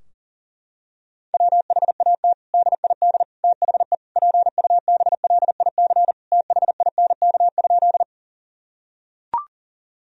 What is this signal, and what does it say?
Morse code, heard as a keyed tone, about 40 wpm, 700 Hz. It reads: WHAT DID THE PUBLIC THINK?
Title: Sentences from Top 1000 Words - Review Part 3 - 40wpm